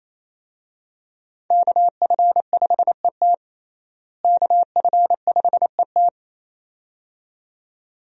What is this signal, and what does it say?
Morse code, heard as a keyed tone, about 27 wpm, 700 Hz. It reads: KF5ET KF5ET
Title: USA Call Signs - 5 Characters - Repeated 2x - 28wpm